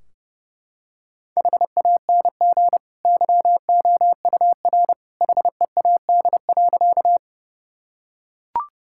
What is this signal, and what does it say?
Morse code, heard as a keyed tone, about 30 wpm, 700 Hz. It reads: HANG YOUR HEAD.
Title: Sentences from Top 1600 Words 30wpm